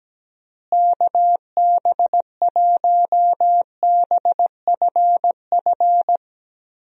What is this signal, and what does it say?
Morse code, heard as a keyed tone, about 17 wpm, 700 Hz. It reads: KB1BFF
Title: USA Call Signs - 6 Characters - Repeated 3x - 17wpm